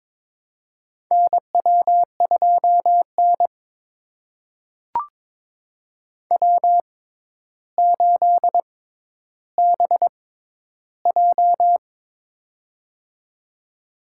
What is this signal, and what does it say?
Morse code, heard as a keyed tone, about 23 wpm, spaced as for 11 wpm, 700 Hz. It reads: NW2N W8BJ
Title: USA Call Signs - 4 Characters - 6x Character Spacing - 22wpm